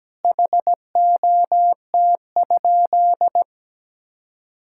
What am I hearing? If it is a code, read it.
Morse code, heard as a keyed tone, about 17 wpm, 700 Hz. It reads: HOT?
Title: Sentences from Top 1000 Words - Review Part 6 - 17wpm